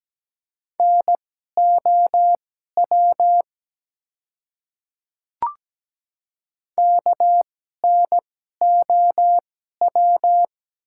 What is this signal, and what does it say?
Morse code, heard as a keyed tone, about 17 wpm, spaced as for 12 wpm, 700 Hz. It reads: NOW KNOW
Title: 2x Character Spacing - Top 100 Words - 17wpm